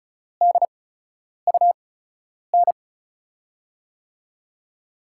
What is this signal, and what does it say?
Morse code, heard as a keyed tone, about 34 wpm, spaced as for 10 wpm, 700 Hz. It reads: DUN
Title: USA Call Signs - 5 Characters - 8x Character Spacing - 35wpm